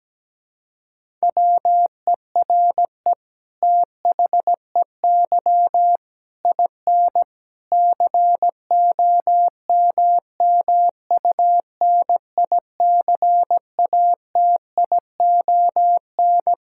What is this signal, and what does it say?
Morse code, heard as a keyed tone, about 17 wpm, 700 Hz. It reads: WERE THEY IN COMMUNICATION
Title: Sentences from Top 1000 Words - Review Part 6 - 17wpm